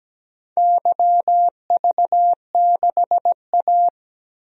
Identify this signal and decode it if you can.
Morse code, heard as a keyed tone, about 17 wpm, 700 Hz. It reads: YV6A
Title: DXCC-100 Call Signs - 4 Characters - 17wpm